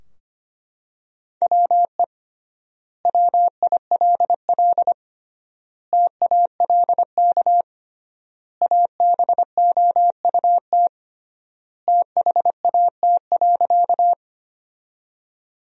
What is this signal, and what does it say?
Morse code, heard as a keyed tone, about 25 wpm, 700 Hz. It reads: WE WILL TALK ABOUT THAT.
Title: 3x Word-Spacing - Sentences from Top 400 Words - 25wpm